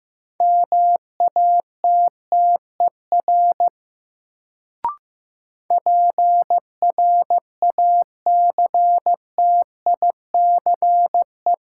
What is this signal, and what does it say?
Morse code, heard as a keyed tone, about 15 wpm, 700 Hz. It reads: MATTER PRACTICE
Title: Sentences from Top 700 Words - New Word Drill - 15wpm